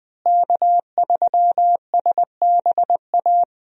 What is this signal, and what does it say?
Morse code, heard as a keyed tone, about 20 wpm, 700 Hz. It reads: K3SBA
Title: USA Call Signs - 5 Characters - Repeated 2x - 20wpm